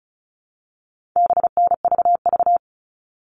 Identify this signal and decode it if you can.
Morse code, heard as a keyed tone, about 34 wpm, 700 Hz. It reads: BNVV
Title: All Cut Numbers - 4-Digit 35wpm